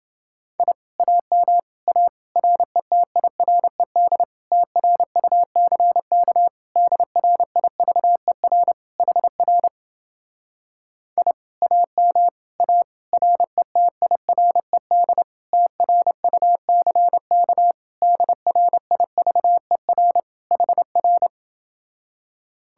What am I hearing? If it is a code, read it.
Morse code, heard as a keyed tone, about 30 wpm, 700 Hz. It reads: I AM A RETIRED TRUCK DRIVER HR I AM A RETIRED TRUCK DRIVER HR